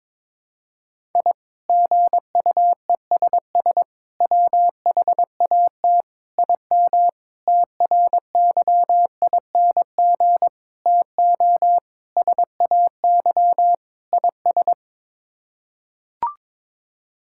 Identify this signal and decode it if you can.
Morse code, heard as a keyed tone, about 22 wpm, 700 Hz. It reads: I GUESS WHAT IM TRYING TO SAY IS
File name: practice / 22wpm / Sets of 8 Words - 22wpm